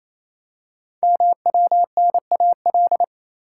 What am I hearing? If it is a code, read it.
Morse code, heard as a keyed tone, about 27 wpm, 700 Hz. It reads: MWNAL
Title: Code Groups - 5 Letters - 28wpm